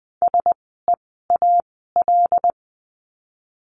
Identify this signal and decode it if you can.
Morse code, heard as a keyed tone, about 20 wpm, spaced as for 14 wpm, 700 Hz. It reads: SEAL
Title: 075 Lesson 13f - Letter L - Words 20wpm